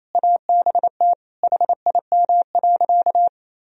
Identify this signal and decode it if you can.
Morse code, heard as a keyed tone, about 28 wpm, 700 Hz. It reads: ABT HIM.
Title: Sentences from Top 200 Words and Abbreviations 28wpm